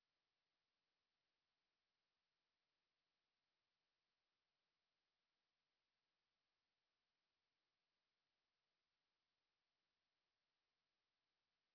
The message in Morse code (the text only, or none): none